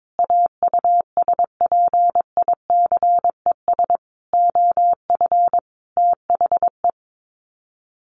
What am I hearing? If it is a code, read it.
Morse code, heard as a keyed tone, about 22 wpm, 700 Hz. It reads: AUSPICES OF THE
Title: Sets of 5 Words Encore2 22wpm